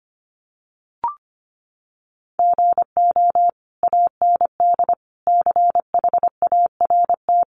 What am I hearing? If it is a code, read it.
Morse code, heard as a keyed tone, about 25 wpm, 700 Hz. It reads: GO AND CHART
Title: Sentences from Top 1300 Words 25wpm